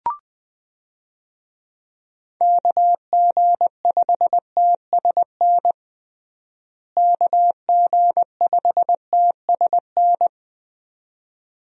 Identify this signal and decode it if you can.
Morse code, heard as a keyed tone, about 20 wpm, 700 Hz. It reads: KG5TSN KG5TSN
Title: USA Call Signs - 6 Characters - Repeated 2x - 20wpm